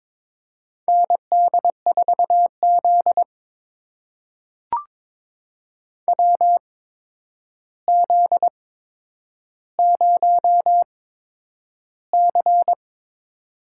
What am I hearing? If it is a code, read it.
Morse code, heard as a keyed tone, about 22 wpm, spaced as for 9 wpm, 700 Hz. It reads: ND4Z WZ0C
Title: USA Call Signs - 4 Characters - 8x Character Spacing - 22wpm